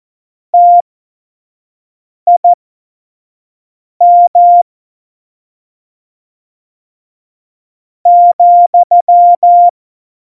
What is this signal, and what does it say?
Morse code, heard as a keyed tone, about 14 wpm, spaced as for 4 wpm, 700 Hz. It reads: TIM,
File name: Morsekurs / Witze / Witz 16. Juli 2019, 25 F70